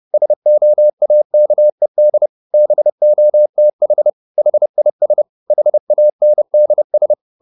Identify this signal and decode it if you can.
Morse code, heard as a keyed tone, about 30 wpm, 600 Hz. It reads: SOAKED BOTH HIS HANDS